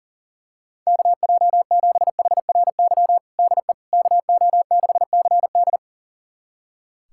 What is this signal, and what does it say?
Morse code, heard as a keyed tone, about 37 wpm, 700 Hz. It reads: KJ7HRY DE KO6CD
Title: USA Callsign DE Callsign 40wpm